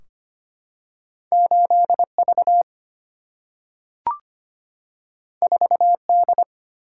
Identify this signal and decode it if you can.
Morse code, heard as a keyed tone, about 25 wpm, 700 Hz. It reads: 8V 4D
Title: Single-Letter Single-Number 25wpm